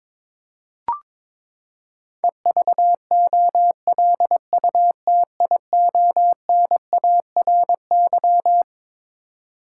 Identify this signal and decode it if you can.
Morse code, heard as a keyed tone, about 22 wpm, 700 Hz. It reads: EVOLUTIONARY
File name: Twelve Letter Words - 22wpm